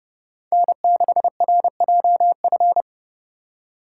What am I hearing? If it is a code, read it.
Morse code, heard as a keyed tone, about 30 wpm, 700 Hz. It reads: N6RJF
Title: USA Call Signs - 5 Characters - 3x Character Spacing - 30wpm